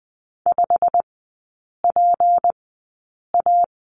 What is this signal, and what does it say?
Morse code, heard as a keyed tone, about 20 wpm, spaced as for 8 wpm, 700 Hz. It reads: 5PA
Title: 172 Lesson 28e - Number 9 - Three Characters 20wpm